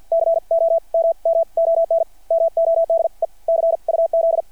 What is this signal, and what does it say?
Morse code, heard as a keyed tone, about 53 wpm, spaced as for 42 wpm, 660 Hz. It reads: COMMON MODE CUZ
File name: archived / CRSnet